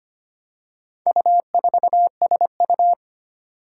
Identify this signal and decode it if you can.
Morse code, heard as a keyed tone, about 25 wpm, 700 Hz. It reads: U4SU
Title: Four Challenging Letter-Number Combinations 25wpm